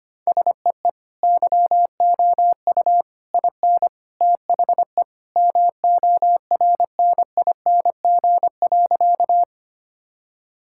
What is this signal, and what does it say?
Morse code, heard as a keyed tone, about 25 wpm, 700 Hz. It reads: SEE YOU IN THE MORNING.
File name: practice / 25wpm / Sentences with 3-Word Phrases - 25wpm